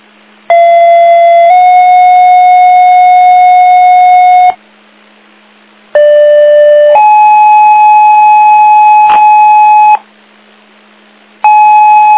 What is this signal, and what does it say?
Two-tone page, heard as a one-second tone, then a three-second tone, about 680 then 730 Hz, about 590 then 850 Hz. It is heard from 0.5 to 10.0 s.